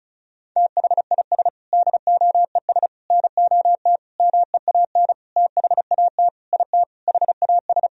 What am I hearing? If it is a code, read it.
Morse code, heard as a keyed tone, about 36 wpm, 700 Hz. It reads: THIS DOES NOT MEAN THAT IT HAS
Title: Sentences from Top 1000 Words - Review Part 4 - 35wpm